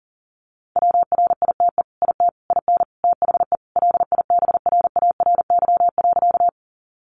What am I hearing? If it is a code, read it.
Morse code, heard as a keyed tone, about 40 wpm, 700 Hz. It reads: WRITE IT IN THE LIBRARY.
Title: Sentences from Top 600 Words 40wpm